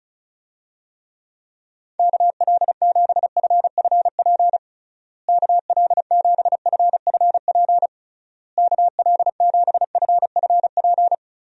Morse code, heard as a keyed tone, about 34 wpm, 700 Hz. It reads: KL7FFP KL7FFP KL7FFP
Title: USA Call Signs - 6 Characters - Repeated 3x - 35wpm